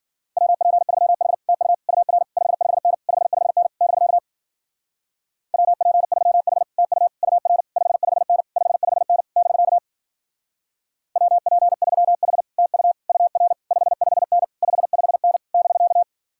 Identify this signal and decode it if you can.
Morse code, heard as a keyed tone, about 50 wpm, 700 Hz. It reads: WP3H TU UR 55N 55N <BK> WP3H TU UR 55N 55N <BK> WP3H TU UR 55N 55N <BK>
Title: Speed-Racing Common QSO Phrases Encore - 40wpm